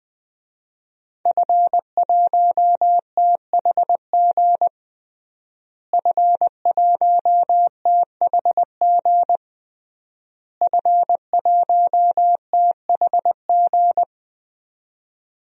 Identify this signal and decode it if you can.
Morse code, heard as a keyed tone, about 21 wpm, 700 Hz. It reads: F1THG F1THG F1THG
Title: DXCC-100 Call Signs - 5 Characters - Repeated 3x - 20wpm